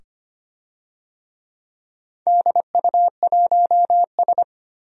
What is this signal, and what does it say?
Morse code, heard as a keyed tone, about 25 wpm, 700 Hz. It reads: DU1S
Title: DXCC-100 Call Signs - 4 Characters - 25wpm